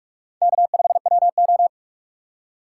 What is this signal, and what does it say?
Morse code, heard as a keyed tone, about 45 wpm, 700 Hz. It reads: K5WO
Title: USA Call Signs - 4x Character Spacing - 45wpm